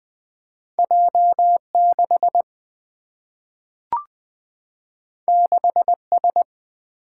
Morse code, heard as a keyed tone, about 20 wpm, 700 Hz. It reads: J6 6S
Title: Single-Letter Single-Number 20wpm